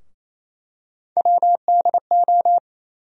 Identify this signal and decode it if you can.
Morse code, heard as a keyed tone, about 28 wpm, 700 Hz. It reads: WDO
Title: USA Call Sign Postfixes - 3 Letters 28wpm